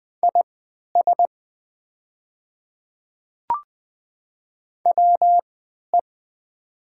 Morse code, heard as a keyed tone, about 20 wpm, spaced as for 11 wpm, 700 Hz. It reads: IS WE
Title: 3x Character Spacing - Top 100 Words - 20wpm